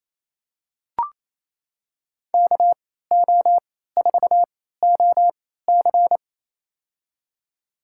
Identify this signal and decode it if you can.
Morse code, heard as a keyed tone, about 28 wpm, spaced as for 16 wpm, 700 Hz. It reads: KO4OC
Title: USA Call Signs - 3x Character Spacing - 28wpm